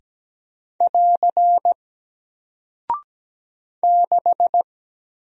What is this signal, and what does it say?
Morse code, heard as a keyed tone, about 17 wpm, 700 Hz. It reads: <AR> 6
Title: Single Character (Letter:Number:Punctuation:Prosign) Alt 17wpm